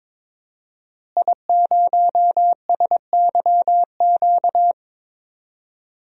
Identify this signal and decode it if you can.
Morse code, heard as a keyed tone, about 21 wpm, 700 Hz. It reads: I0SYQ
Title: DXCC-100 Call Signs - 5 Characters - Repeated 3x - 22wpm